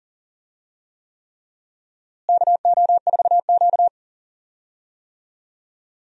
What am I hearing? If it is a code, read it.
Morse code, heard as a keyed tone, about 40 wpm, 700 Hz. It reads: KO4Q
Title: USA Call Signs - 4 Characters - Repeated 3x - 40wpm